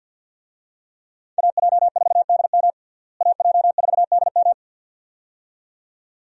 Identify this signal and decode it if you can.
Morse code, heard as a keyed tone, about 52 wpm, 700 Hz. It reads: AJ4DM AJ4DM
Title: USA Call Signs - 5 Characters - Repeated 2x - 50wpm